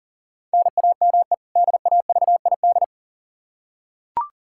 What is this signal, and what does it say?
Morse code, heard as a keyed tone, about 40 wpm, 700 Hz. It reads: NAME DAVID
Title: K1USN SST QSO Practice 40wpm